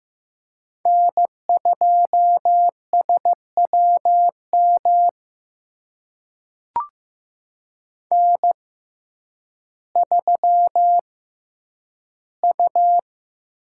Morse code, heard as a keyed tone, about 15 wpm, spaced as for 9 wpm, 700 Hz. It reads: N2SWM N3U
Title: USA Call Signs - 5 Characters - 6x Character Spacing - 15wpm